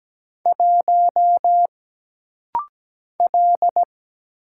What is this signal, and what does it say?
Morse code, heard as a keyed tone, about 17 wpm, 700 Hz. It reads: 1 L
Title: Single Letter-Number - Rapid-Fire 17wpm